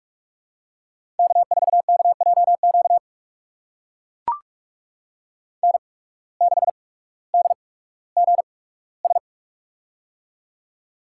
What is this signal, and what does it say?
Morse code, heard as a keyed tone, about 43 wpm, spaced as for 19 wpm, 700 Hz. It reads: K4KJQ N6DGS
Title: USA Call Signs - 5 Characters - 8x Character Spacing - 45wpm